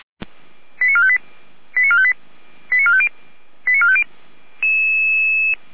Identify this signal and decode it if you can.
Five-tone selective call (ZVEI), heard as quick runs of five tones, tones of about 70 ms. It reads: 89458 89458 89450 89450
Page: SDR/ZVEI